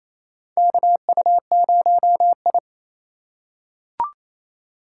Morse code, heard as a keyed tone, about 28 wpm, 700 Hz. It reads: KU0I
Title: USA Call Signs - 4 Characters - 3x Character Spacing - 28wpm